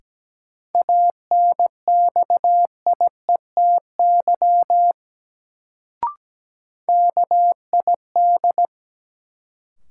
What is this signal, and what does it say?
Morse code, heard as a keyed tone, about 17 wpm, 700 Hz. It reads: ANXIETY KID
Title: Sentences from Top 1600 Words - New Word Drill Alt - 17wpm